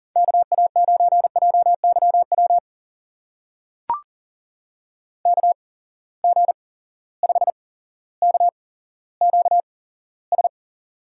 Morse code, heard as a keyed tone, about 40 wpm, spaced as for 17 wpm, 700 Hz. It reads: KA9JYW KG5KQS
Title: USA Call Signs - 6 Characters - 8x Character Spacing - 40wpm